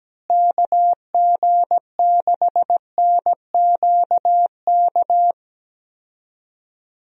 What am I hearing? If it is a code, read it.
Morse code, heard as a keyed tone, about 17 wpm, 700 Hz. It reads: KG6NQK
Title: USA Call Signs - 6 Characters - Repeated 2x - 17wpm